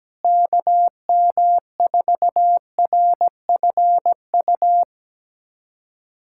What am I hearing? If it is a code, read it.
Morse code, heard as a keyed tone, about 17 wpm, 700 Hz. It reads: KM4RFU
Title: USA Callsign DE Callsign 17wpm